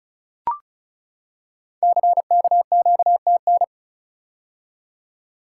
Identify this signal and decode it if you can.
Morse code, heard as a keyed tone, about 36 wpm, 700 Hz. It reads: CKQTN
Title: Code Groups - 5 Letters - 35wpm